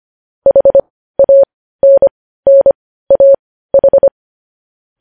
Morse code, heard as a keyed tone, about 25 wpm, spaced as for 15 wpm, 550 Hz. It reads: HANNAH
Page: news headlines in morse code at 15 wpm